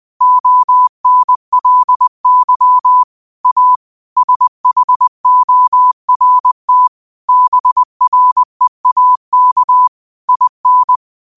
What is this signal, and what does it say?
Morse code, heard as a keyed tone, about 20 wpm, 1000 Hz. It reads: ONLY A SHORT BREAK IN